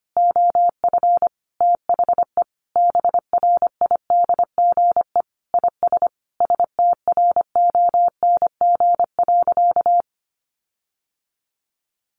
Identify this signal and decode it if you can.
Morse code, heard as a keyed tone, about 25 wpm, 700 Hz. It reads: OF THE BRIDGE IS STRONG.
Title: Sentences from Top 1400 Words 25wpm